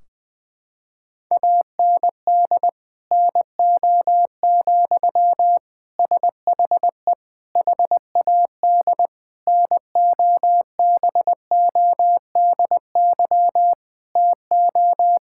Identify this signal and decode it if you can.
Morse code, heard as a keyed tone, about 20 wpm, 700 Hz. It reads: AND NO, SHE HAD NOBODY TO
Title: Sentences from Top 1600 Words 20wpm